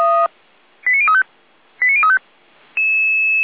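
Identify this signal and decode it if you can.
Five-tone selective call (ZVEI), heard as quick runs of five tones, tones of about 70 ms. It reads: 89035 89035
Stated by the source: ZVEI five-tone selective call 89042 received on an SDR.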